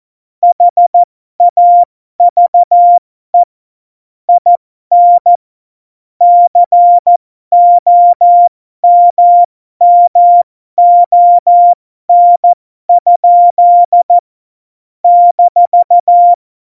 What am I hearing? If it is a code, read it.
Morse code, heard as a keyed tone, about 14 wpm, spaced as for 12 wpm, 700 Hz. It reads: HAVE IN COMMON? -
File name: Morsekurs / Witze